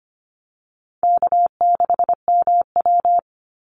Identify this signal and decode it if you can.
Morse code, heard as a keyed tone, about 25 wpm, 700 Hz. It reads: K6MW
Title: NA SOTA Chaser Callsigns 25wpm